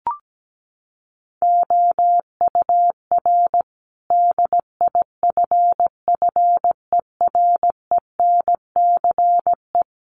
Morse code, heard as a keyed tone, about 17 wpm, 700 Hz. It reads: OUR DIFFERENCE
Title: Sentences from Top 1500 Words 17wpm